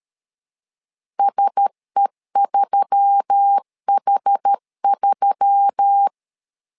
Morse code, heard as a keyed tone, about 13 wpm, 800 Hz. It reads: SE3H3
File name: LEZIONI / 4 Q1S3 12wpm 2023